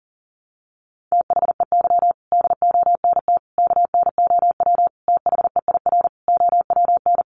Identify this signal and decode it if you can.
Morse code, heard as a keyed tone, about 40 wpm, 700 Hz. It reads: THEY DONT KNOW THEIR OWN
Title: Sentences from Top 300 Words 40wpm